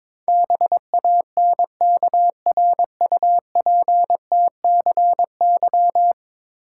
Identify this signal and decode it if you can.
Morse code, heard as a keyed tone, about 22 wpm, 700 Hz. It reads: BANKRUPTCY
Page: morse-code-ninja-practice-22wpm